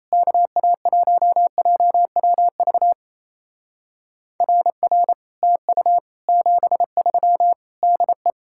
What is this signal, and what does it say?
Morse code, heard as a keyed tone, about 38 wpm, 700 Hz. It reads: KA1JWV RR TU 73 DE